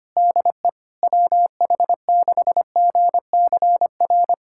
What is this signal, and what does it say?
Morse code, heard as a keyed tone, about 25 wpm, 700 Hz. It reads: DE WH6GCR
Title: USA Callsign DE Callsign - 6 Characters - 25wpm